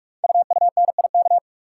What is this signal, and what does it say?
Morse code, heard as a keyed tone, about 44 wpm, 690 Hz. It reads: UUNIK
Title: Code Groups - 5 Letters - 45wpm